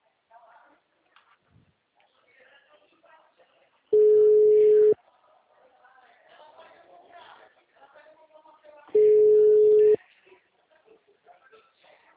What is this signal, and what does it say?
Ringback tone (Europe, 425 Hz), with speech.